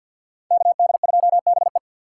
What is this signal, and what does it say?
Morse code, heard as a keyed tone, about 51 wpm, 700 Hz. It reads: KD2BE
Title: USA Call Signs - 5 Characters - 3x Character Spacing - 50wpm